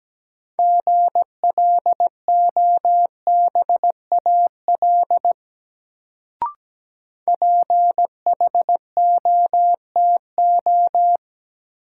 Morse code, heard as a keyed tone, about 17 wpm, 700 Hz. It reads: GLOBAL PHOTO
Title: Sentences from Top 1200 Words - New Word Drill - 17wpm